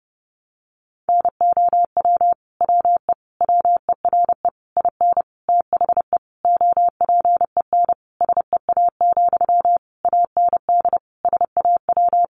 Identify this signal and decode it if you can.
Morse code, heard as a keyed tone, about 30 wpm, 700 Hz. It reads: NOW WE WERE IN THE OPEN SEA, AND SAW
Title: Sentences from Top 1200 Words 30wpm